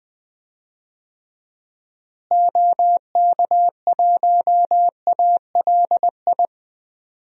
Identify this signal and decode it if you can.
Morse code, heard as a keyed tone, about 20 wpm, 700 Hz. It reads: OK1ALI
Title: DXCC-100 Call Signs - 6 Characters - Repeated 3x - 20wpm